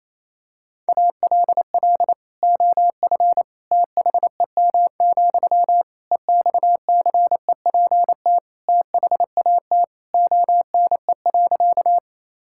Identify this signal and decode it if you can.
Morse code, heard as a keyed tone, about 28 wpm, 700 Hz. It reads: ALL OF THEM, EXCEPT THAT ONE.
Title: Sentences from Top 1600 Words 28wpm